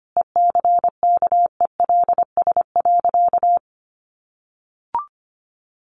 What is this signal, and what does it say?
Morse code, heard as a keyed tone, about 25 wpm, 700 Hz. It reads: ECKELS.
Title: A Sound Of Thunder 25wpm